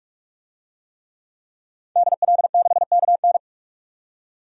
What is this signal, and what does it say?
Morse code, heard as a keyed tone, about 41 wpm, 700 Hz. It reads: DL6KN
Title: DXCC-100 Call Signs - 5 Characters - 45wpm